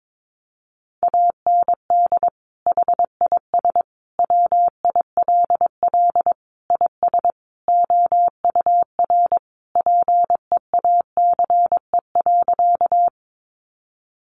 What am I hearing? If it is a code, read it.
Morse code, heard as a keyed tone, about 22 wpm, 700 Hz. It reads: AND HIS WILL IS OUR PEACE.
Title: Sentences from Top 1500 Words 22wpm